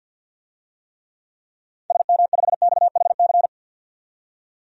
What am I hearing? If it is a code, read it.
Morse code, heard as a keyed tone, about 52 wpm, 700 Hz. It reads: SM5XHC
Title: DXCC-100 Call Signs - 6 Characters - 50wpm